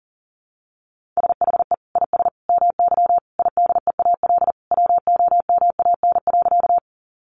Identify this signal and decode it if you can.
Morse code, heard as a keyed tone, about 40 wpm, 700 Hz. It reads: SHE IS MY IDEAL WOMAN.
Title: Sentences from Top 1400 Words 40wpm